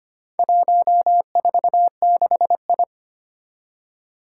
Morse code, heard as a keyed tone, about 25 wpm, 700 Hz. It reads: 146I
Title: Four Challenging Letter-Number Combinations 25wpm